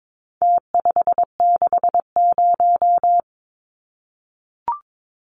Morse code, heard as a keyed tone, about 22 wpm, 700 Hz. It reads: T560
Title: Numbers - Contest Style - 4-Digit 22wpm